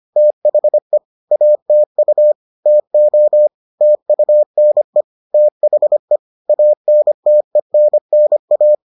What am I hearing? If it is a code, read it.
Morse code, heard as a keyed tone, about 25 wpm, 600 Hz. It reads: THE ATU TO TUNE THE ANTENNA